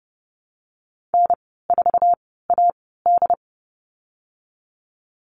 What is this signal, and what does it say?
Morse code, heard as a keyed tone, about 30 wpm, spaced as for 17 wpm, 700 Hz. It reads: N4AD
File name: Lesson 14g - Number 2 - Call Signs 30wpm